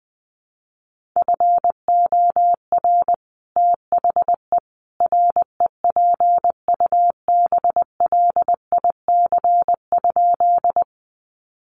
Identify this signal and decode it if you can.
Morse code, heard as a keyed tone, about 20 wpm, 700 Hz. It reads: FOR THE REPUBLIC?
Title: Sentences from Top 1500 Words 20wpm